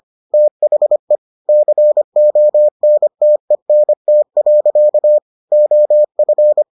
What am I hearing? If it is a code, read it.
Morse code, heard as a keyed tone, about 25 wpm, 600 Hz. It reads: THE CONTENT. OF